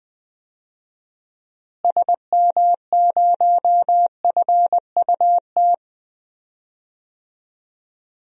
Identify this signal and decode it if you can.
Morse code, heard as a keyed tone, about 20 wpm, 700 Hz. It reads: SM0FUT
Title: DXCC-100 Call Signs - 6 Characters - Repeated 2x - 20wpm